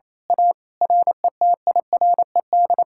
Morse code, heard as a keyed tone, about 26 wpm, 700 Hz. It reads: A RETIRED
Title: Speed-Racing Common QSO Phrases - 28wpm